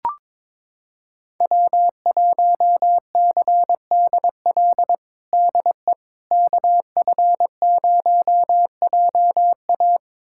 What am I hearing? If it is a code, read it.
Morse code, heard as a keyed tone, about 22 wpm, 700 Hz. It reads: W1CDL DE KF0JA
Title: USA Callsign DE Callsign - 5 Characters - 22wpm